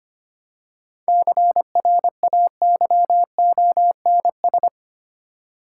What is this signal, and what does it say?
Morse code, heard as a keyed tone, about 25 wpm, 690 Hz. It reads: CRAYONS
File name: practice / 25wpm / Easy Words with Hints - 25wpm